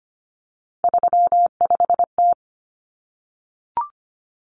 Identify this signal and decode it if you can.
Morse code, heard as a keyed tone, about 25 wpm, 700 Hz. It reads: 35T
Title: Common Cut Numbers - 3-Digit 25wpm